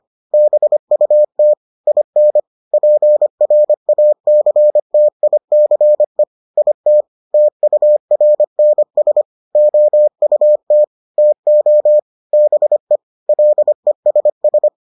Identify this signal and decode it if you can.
Morse code, heard as a keyed tone, about 25 wpm, 600 Hz. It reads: BUT IN PRACTICE IT TURNS OUT TO BE LESS